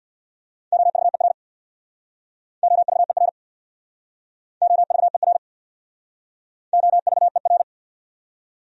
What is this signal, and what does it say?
Morse code, heard as a keyed tone, about 69 wpm, 700 Hz. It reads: OVER OVER OVER OVER